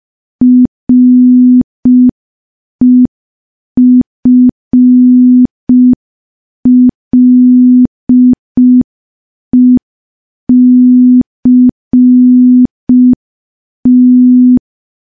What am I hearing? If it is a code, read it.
Morse code, heard as a keyed tone, about 5 wpm, 250 Hz. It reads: REFLECT
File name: morse / morse-training-05WPM